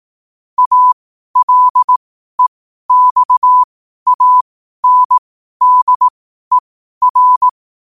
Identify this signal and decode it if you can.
Morse code, heard as a keyed tone, about 18 wpm, spaced as for 12 wpm, 1000 Hz. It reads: ALEXANDER
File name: qotdfiles / 12wpm